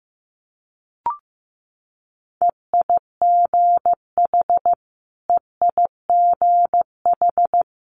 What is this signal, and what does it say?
Morse code, heard as a keyed tone, about 15 wpm, 700 Hz. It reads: EIGH EIGH
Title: Multi-letter Phonemes Repeated 3x 15wpm